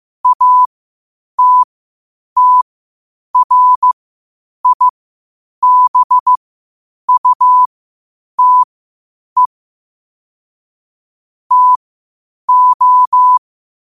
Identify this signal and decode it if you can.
Morse code, heard as a keyed tone, about 15 wpm, spaced as for 8 wpm, 1000 Hz. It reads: ATTRIBUTE TO
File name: 8wpm